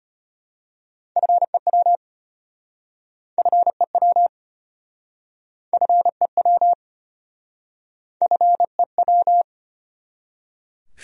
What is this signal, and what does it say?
Morse code, heard as a keyed tone, about 33 wpm, 700 Hz. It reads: FEW FEW FEW FEW